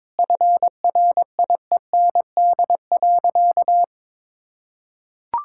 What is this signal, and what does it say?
Morse code, heard as a keyed tone, about 21 wpm, 700 Hz. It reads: FRIEND.
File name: practice / 22wpm / Sentences with 2-Word Phrases - 22wpm